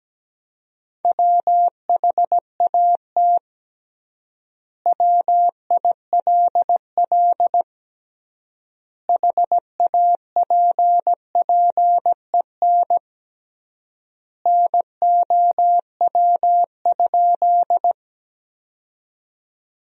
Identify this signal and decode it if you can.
Morse code, heard as a keyed tone, about 17 wpm, 700 Hz. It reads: WHAT WILL HAPPEN NOW?
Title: 3x Word-Spacing - Sentences from Top 400 Words - 17wpm